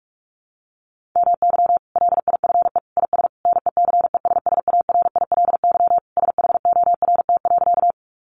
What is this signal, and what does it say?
Morse code, heard as a keyed tone, about 44 wpm, 700 Hz. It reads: MY LIFE IS NECESSARILY SHORT.
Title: Sentences from Top 1400 Words 45wpm